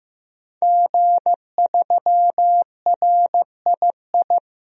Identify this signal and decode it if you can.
Morse code, heard as a keyed tone, about 15 wpm, 700 Hz. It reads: G3RII
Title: DXCC-100 Call Signs - 5 Characters - 15wpm